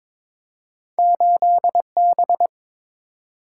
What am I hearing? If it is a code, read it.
Morse code, heard as a keyed tone, about 22 wpm, 700 Hz. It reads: 8B
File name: practice / 22wpm / Two Letter-Number Combinations - 22wpm